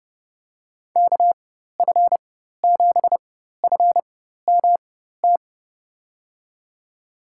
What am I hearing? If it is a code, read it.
Morse code, heard as a keyed tone, about 32 wpm, spaced as for 14 wpm, 700 Hz. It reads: KF7FMT
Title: USA Call Signs - 6 Characters - 4x Character Spacing - 30wpm